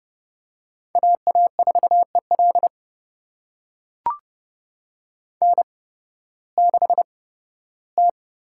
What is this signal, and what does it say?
Morse code, heard as a keyed tone, about 29 wpm, spaced as for 16 wpm, 700 Hz. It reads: AA4EL N6T
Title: USA Call Signs - 5 Characters - 8x Character Spacing - 30wpm